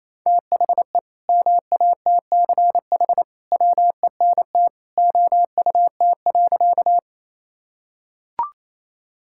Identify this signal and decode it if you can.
Morse code, heard as a keyed tone, about 28 wpm, 700 Hz. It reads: THE MATCH WENT OUT.
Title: Sentences from Top 1600 Words 28wpm